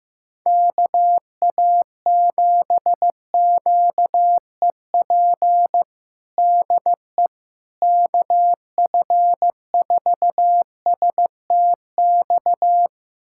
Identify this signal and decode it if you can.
Morse code, heard as a keyed tone, about 15 wpm, 700 Hz. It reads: KA7QEP DE KF4STX
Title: USA Callsign DE Callsign 15wpm